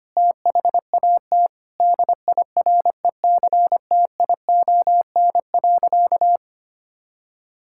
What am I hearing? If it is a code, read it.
Morse code, heard as a keyed tone, about 25 wpm, 700 Hz. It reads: THAT DIRECTION.
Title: Sentences from Top 1000 Words - Review Part 6 - 25wpm